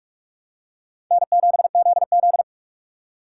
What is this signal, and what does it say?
Morse code, heard as a keyed tone, about 46 wpm, 700 Hz. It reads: N7ZZ
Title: USA Call Signs - 4 Characters - 6x Character Spacing - 45wpm